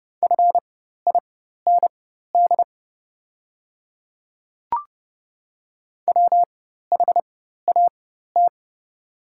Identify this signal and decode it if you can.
Morse code, heard as a keyed tone, about 31 wpm, spaced as for 14 wpm, 700 Hz. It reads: FIND WHAT
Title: 4x Character Spacing - Top 200 Words - 30wpm